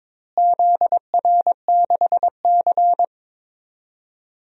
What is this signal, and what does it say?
Morse code, heard as a keyed tone, about 21 wpm, 700 Hz. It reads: ZR6C
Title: DXCC-100 Call Signs - 4 Characters - Repeated 3x - 22wpm